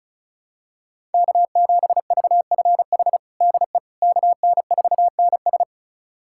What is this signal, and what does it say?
Morse code, heard as a keyed tone, about 36 wpm, 700 Hz. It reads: K7VFH DE KN4NS
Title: USA Callsign DE Callsign - 5 Characters - 35wpm